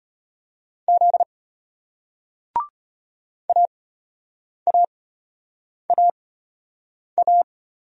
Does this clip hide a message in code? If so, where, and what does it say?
Yes: Morse code from 0.9 to 7.4 s, heard as a keyed tone, about 35 wpm, spaced as for 8 wpm, 700 Hz. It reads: Z AAAA